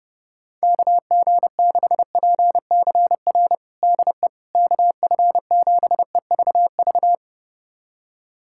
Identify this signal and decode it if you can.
Morse code, heard as a keyed tone, about 32 wpm, 700 Hz. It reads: KG6PCR DE KF7EVV
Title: USA Callsign DE Callsign - 6 Characters - 30wpm